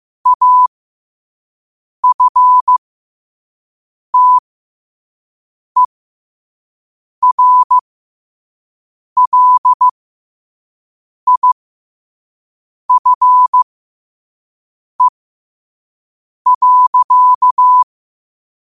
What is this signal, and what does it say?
Morse code, heard as a keyed tone, about 15 wpm, spaced as for 5 wpm, 1000 Hz. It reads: AFTERLIFE.